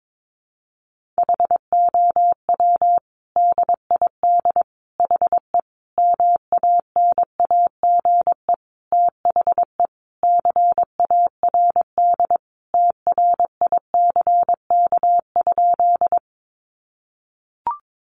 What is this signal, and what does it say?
Morse code, heard as a keyed tone, about 22 wpm, 700 Hz. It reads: HOW DID HE MANAGE THE CARD TRICK?
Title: Sentences from Top 1500 Words 22wpm